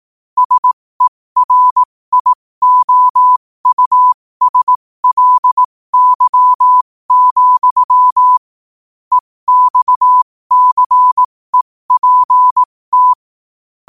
Morse code, heard as a keyed tone, about 18 wpm, spaced as for 15 wpm, 1000 Hz. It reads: SERIOUSLY, EXCEPT
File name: qotdfiles / 15wpm